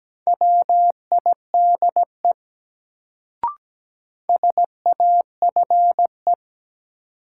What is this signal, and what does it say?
Morse code, heard as a keyed tone, about 17 wpm, 700 Hz. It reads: WIDE SAFE
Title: Sentences from Top 900 Words - New Word Drill - 17wpm